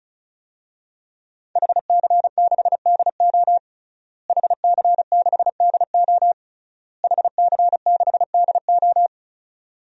Morse code, heard as a keyed tone, about 36 wpm, 700 Hz. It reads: HC6DO HC6DO HC6DO
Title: DXCC-100 Call Signs - 5 Characters - Repeated 3x - 35wpm